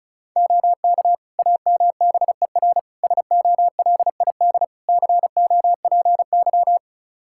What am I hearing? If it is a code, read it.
Morse code, heard as a keyed tone, about 36 wpm, 700 Hz. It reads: OK AMBER SOLID COPY
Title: Practice QSO - Exchange 3 - More Info - 35wpm